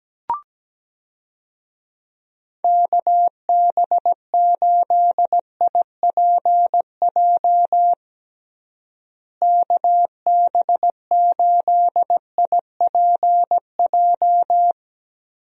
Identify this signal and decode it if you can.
Morse code, heard as a keyed tone, about 17 wpm, 700 Hz. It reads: KB8IPJ KB8IPJ
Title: USA Call Signs - 6 Characters - Repeated 3x - 17wpm